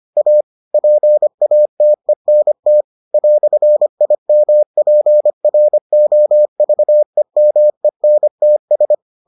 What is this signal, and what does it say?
Morse code, heard as a keyed tone, about 25 wpm, 600 Hz. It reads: A PATENT "IMPROVEMENTS